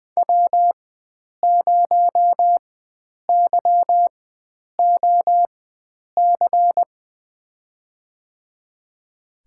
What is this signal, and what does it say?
Morse code, heard as a keyed tone, about 21 wpm, spaced as for 9 wpm, 700 Hz. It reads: W0YOC
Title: USA Call Signs - 5 Characters - 4x Character Spacing - 20wpm